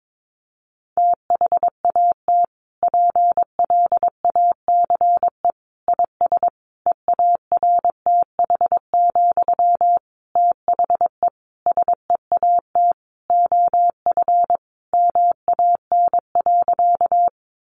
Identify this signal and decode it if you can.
Morse code, heard as a keyed tone, about 22 wpm, 700 Hz. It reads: THAT PLACE IS EARTH, THE SEAT OF MAN.
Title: Sentences from Top 1500 Words 22wpm